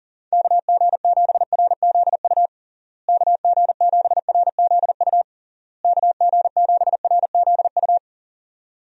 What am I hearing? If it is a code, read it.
Morse code, heard as a keyed tone, about 40 wpm, 700 Hz. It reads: KG7RZU KG7RZU KG7RZU
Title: USA Call Signs - 6 Characters - Repeated 3x - 40wpm